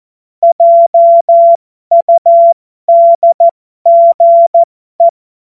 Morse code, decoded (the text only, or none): JUDGE